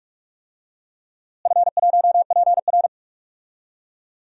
Morse code, heard as a keyed tone, about 46 wpm, 700 Hz. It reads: F1PR
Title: DXCC-100 Call Signs - 4 Characters - Repeated 3x - 45wpm